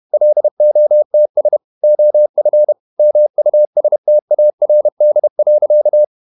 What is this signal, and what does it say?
Morse code, heard as a keyed tone, about 31 wpm, 600 Hz. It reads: LOTS OF MUSTARD.
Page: Never 31wpm | CW med Gnister